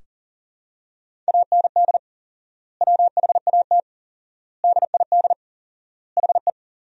Morse code, heard as a keyed tone, about 37 wpm, 700 Hz. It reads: AND WHAT DID HE
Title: 4x Word-Spacing - Sentences from Top 200 Words - 40wpm